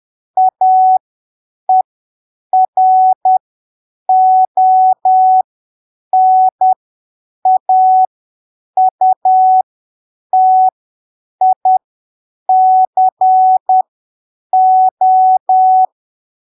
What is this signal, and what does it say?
Morse code, heard as a keyed tone, about 10 wpm, spaced as for 7 wpm, 750 Hz. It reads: AERONAUTICO
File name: arquivos / texto / texto 07 10ppm